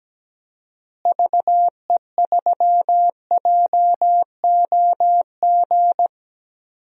Morse code, heard as a keyed tone, about 17 wpm, 700 Hz. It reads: VE3JOG
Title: DXCC-100 Call Signs - 6 Characters - Repeated 3x - 17wpm